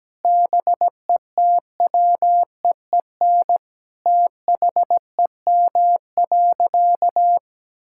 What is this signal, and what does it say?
Morse code, heard as a keyed tone, about 17 wpm, 700 Hz. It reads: BETWEEN THEM.
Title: Sentences from Top 1000 Words - Review Part 6 - 17wpm